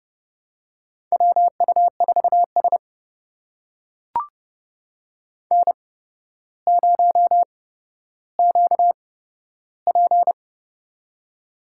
Morse code, heard as a keyed tone, about 31 wpm, spaced as for 13 wpm, 700 Hz. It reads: WU4S N0QP
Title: USA Call Signs - 4 Characters - 8x Character Spacing - 30wpm